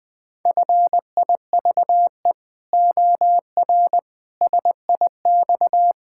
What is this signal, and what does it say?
Morse code, heard as a keyed tone, about 20 wpm, 700 Hz. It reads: FIVE OR SIX
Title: Pangram Challenge v2 - 20wpm